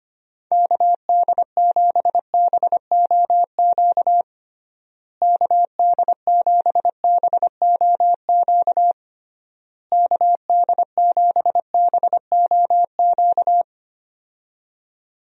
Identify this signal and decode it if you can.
Morse code, heard as a keyed tone, about 25 wpm, 700 Hz. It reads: KD7BOQ KD7BOQ KD7BOQ
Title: USA Call Signs - 6 Characters - Repeated 3x - 25wpm